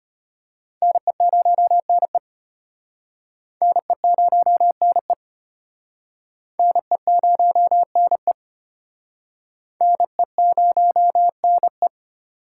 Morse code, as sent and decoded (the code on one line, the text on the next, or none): -. ----- -. / -. ----- -. / -. ----- -. / -. ----- -. .
NE0NE NE0NE NE0NE NE0NE